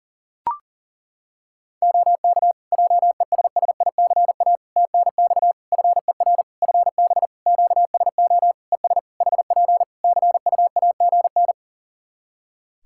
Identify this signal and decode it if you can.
Morse code, heard as a keyed tone, about 40 wpm, 700 Hz. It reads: OK JESSICA TNX FER FB QSO ES HP CUAGN
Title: Practice QSO - Exchange 4 - Ending - 40wpm